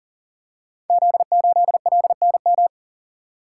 Morse code, heard as a keyed tone, about 41 wpm, 690 Hz. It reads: Z8LNM
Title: Code Groups - 5 Letter:Numbers - 40wpm